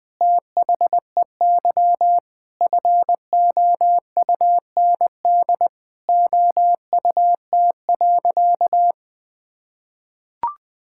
Morse code, heard as a keyed tone, about 20 wpm, 700 Hz. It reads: THEY FOUND OUT.
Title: Sentences with 8-Word Phrases - 20wpm